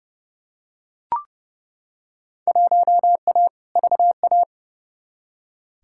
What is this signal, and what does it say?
Morse code, heard as a keyed tone, about 29 wpm, 700 Hz. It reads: 1A VA
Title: Field Day Exchanges Encore - 30wpm